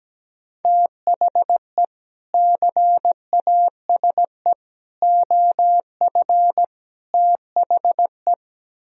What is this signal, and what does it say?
Morse code, heard as a keyed tone, about 17 wpm, 700 Hz. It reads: THE CASE OF THE
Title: Speed-Racing - Sets of 4 Words 17wpm